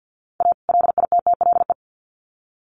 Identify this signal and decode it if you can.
Morse code, heard as a keyed tone, about 50 wpm, 700 Hz. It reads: A LITTLE
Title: Sets of 2 Words 50wpm